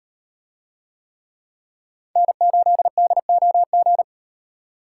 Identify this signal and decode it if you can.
Morse code, heard as a keyed tone, about 36 wpm, 700 Hz. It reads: N8DOG